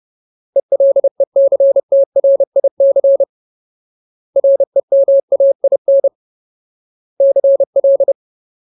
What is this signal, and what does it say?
Morse code, heard as a keyed tone, about 30 wpm, 550 Hz. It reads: ELECTRIC REMAIN CL
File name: Ultimate Word List / 30WPM Ultimate